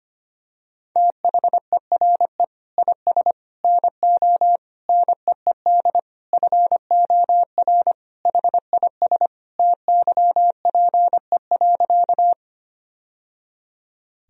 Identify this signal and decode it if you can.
Morse code, heard as a keyed tone, about 25 wpm, 700 Hz. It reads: THERE IS NO NEED FOR HIS TYPE.
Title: Sentences from Top 1000 Words - Review Part 3 - 25wpm